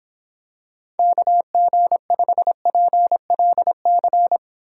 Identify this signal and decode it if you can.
Morse code, heard as a keyed tone, about 26 wpm, 700 Hz. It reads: KG5PLC